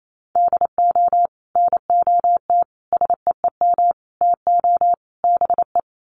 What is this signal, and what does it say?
Morse code, heard as a keyed tone, about 28 wpm, 700 Hz. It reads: DO NOT SEEM TO BE
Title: Sets of 5 Words Encore2 28wpm